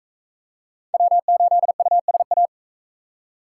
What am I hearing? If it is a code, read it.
Morse code, heard as a keyed tone, about 44 wpm, 700 Hz. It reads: W8USA